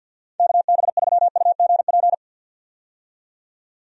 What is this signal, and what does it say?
Morse code, heard as a keyed tone, about 53 wpm, 700 Hz. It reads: KB3UGP
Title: USA Call Signs - 6 Characters - 50wpm